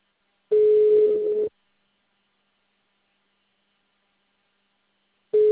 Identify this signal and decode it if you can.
Ringback tone (Europe, 425 Hz).
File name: out- -2016- - - .1148